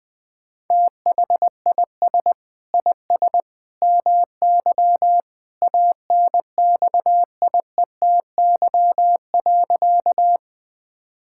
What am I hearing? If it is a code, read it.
Morse code, heard as a keyed tone, about 20 wpm, 700 Hz. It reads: THIS IS MY ANXIETY.
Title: Sentences from Top 1600 Words 20wpm